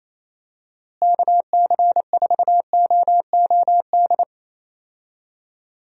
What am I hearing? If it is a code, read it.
Morse code, heard as a keyed tone, about 28 wpm, 700 Hz. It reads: KC4OOD
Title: USA Call Signs - 6 Characters - Repeated 3x - 28wpm